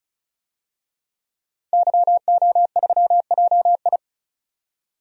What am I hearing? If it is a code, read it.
Morse code, heard as a keyed tone, about 36 wpm, 700 Hz. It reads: YO3JI
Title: DXCC-100 Call Signs - 5 Characters - 35wpm